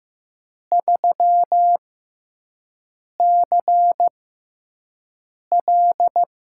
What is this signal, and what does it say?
Morse code, heard as a keyed tone, about 15 wpm, spaced as for 5 wpm, 700 Hz. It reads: 3CL
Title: USA Call Signs - 6x Character Spacing - 15wpm